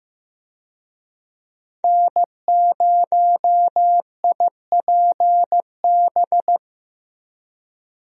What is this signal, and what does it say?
Morse code, heard as a keyed tone, about 15 wpm, 700 Hz. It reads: N0IPB
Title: USA Call Signs - 5 Characters - Repeated 2x - 15wpm